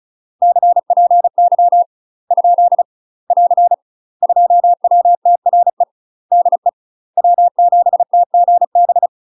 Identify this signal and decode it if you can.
Morse code, heard as a keyed tone, about 36 wpm, 700 Hz. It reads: CPY ? <AR> 2WTRE DE W7TGB
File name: UK MASTER QSOs TRAINING / 35WPM / UK MASTER QSOs 0011